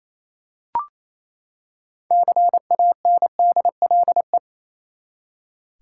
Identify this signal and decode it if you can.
Morse code, heard as a keyed tone, about 28 wpm, 700 Hz. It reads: CANDLE
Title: Six Letter Words - 28wpm